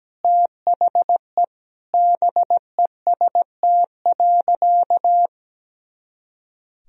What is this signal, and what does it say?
Morse code, heard as a keyed tone, about 17 wpm, 700 Hz. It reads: THE BEST.
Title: Sentences from Top 500 Words and Abbreviations 17wpm